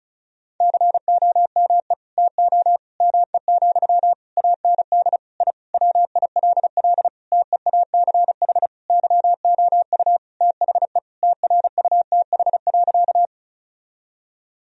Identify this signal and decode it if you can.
Morse code, heard as a keyed tone, about 34 wpm, 700 Hz. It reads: COME TO ME, AND I WILL TEACH YOU THE TRUTH.
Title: Sentences from Top 1000 Words - Review Part 2 - 35wpm